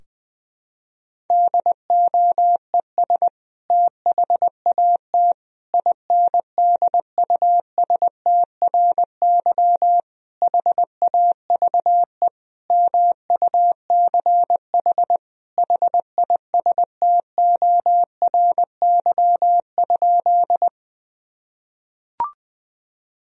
Morse code, heard as a keyed tone, about 20 wpm, 700 Hz. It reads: DOES THAT INDUSTRY HAVE MUCH HISTORY?
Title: Sentences from Top 1000 Words - Review Part 4 - 20wpm